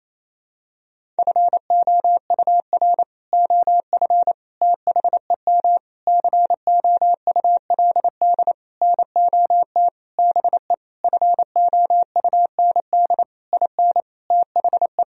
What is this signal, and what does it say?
Morse code, heard as a keyed tone, about 28 wpm, 700 Hz. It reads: FOUR OF THEM COULD NOT BE FOUND IN THE
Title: Sentences from Top 1000 Words - Review Part 1 - 28wpm